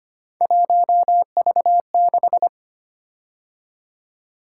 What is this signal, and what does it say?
Morse code, heard as a keyed tone, about 25 wpm, 700 Hz. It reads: 1V6
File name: Three Letter-Number Combinations - 25wpm